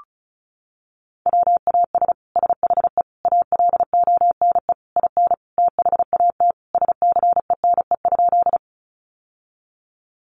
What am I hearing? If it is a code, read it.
Morse code, heard as a keyed tone, about 34 wpm, 700 Hz. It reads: WAS SHE ALONE IN THAT SCENE?
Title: Sentences from Top 1100 Words 35wpm